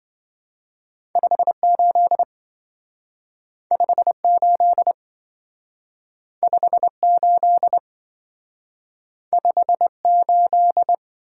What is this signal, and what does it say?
Morse code, heard as a keyed tone, about 25 wpm, 700 Hz. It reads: 58 58 58 58
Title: Speed-Racing - Two Letter-Number Combinations - 20wpm